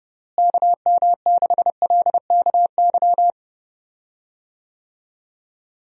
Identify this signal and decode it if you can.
Morse code, heard as a keyed tone, about 32 wpm, 700 Hz. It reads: KM6LKY
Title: USA Call Signs - 6 Characters - Repeated 2x - 30wpm